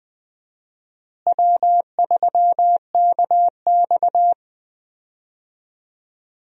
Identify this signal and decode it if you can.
Morse code, heard as a keyed tone, about 20 wpm, 700 Hz. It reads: W3KX
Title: USA Call Signs - 4 Characters - Repeated 2x - 20wpm